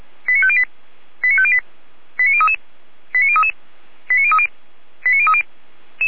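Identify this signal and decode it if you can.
Five-tone selective call (ZVEI), heard as quick runs of five tones, tones of about 70 ms. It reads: 89508 89508 89033 89033 89030 89030
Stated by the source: ZVEI five-tone call received on an SDR by Feuerwehr Orlamünde.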